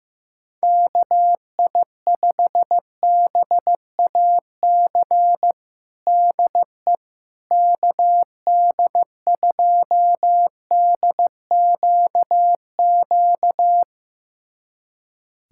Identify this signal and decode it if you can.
Morse code, heard as a keyed tone, about 15 wpm, 700 Hz. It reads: KI5BAC DE KD2DQQ
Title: USA Callsign DE Callsign - 6 Characters - 15wpm